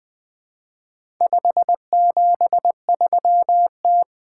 Morse code, heard as a keyed tone, about 20 wpm, 700 Hz. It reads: 573T
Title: Four Letter-Number Combinations - 20wpm